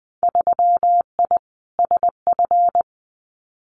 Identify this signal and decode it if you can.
Morse code, heard as a keyed tone, about 20 wpm, 700 Hz. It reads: 3I SF
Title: Winter Field Day Exchanges 20wpm